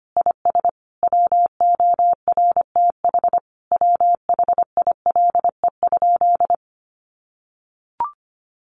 Morse code, heard as a keyed tone, about 25 wpm, 700 Hz. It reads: IS WORTH WHILE?
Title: Sentences from Top 800 Words 25wpm